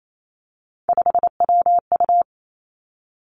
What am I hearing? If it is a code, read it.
Morse code, heard as a keyed tone, about 28 wpm, 700 Hz. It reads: 5WU
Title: Three Challenging Letter-Number Combinations 28wpm